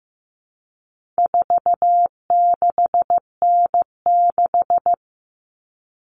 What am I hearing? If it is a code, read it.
Morse code, heard as a keyed tone, about 15 wpm, 700 Hz. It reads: 46N6
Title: Common Cut Numbers - 4-Digit 15wpm